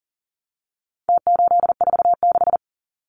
Morse code, heard as a keyed tone, about 40 wpm, 700 Hz. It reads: T846
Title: Numbers - Contest Style - 4-Digit 40wpm